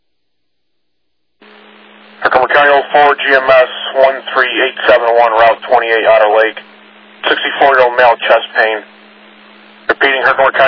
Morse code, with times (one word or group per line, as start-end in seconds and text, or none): none